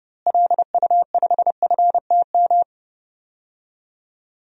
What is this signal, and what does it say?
Morse code, heard as a keyed tone, about 30 wpm, 700 Hz. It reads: LU5FTM
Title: DXCC-100 Call Signs - 6 Characters - Repeated 2x - 30wpm